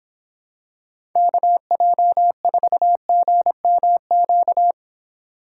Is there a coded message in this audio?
Yes: Morse code from 1.2 to 4.7 s, heard as a keyed tone, about 25 wpm, 700 Hz. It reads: KJ4GMQ